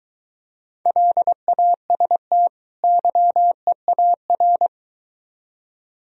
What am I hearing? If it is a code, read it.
Morse code, heard as a keyed tone, about 22 wpm, 700 Hz. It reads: LAST YEAR